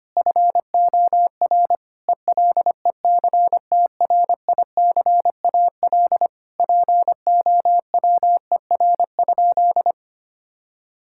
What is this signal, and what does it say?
Morse code, heard as a keyed tone, about 25 wpm, 700 Hz. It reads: FOR ELECTRICAL POWER?
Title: Sentences from Top 1000 Words - Review Part 6 - 25wpm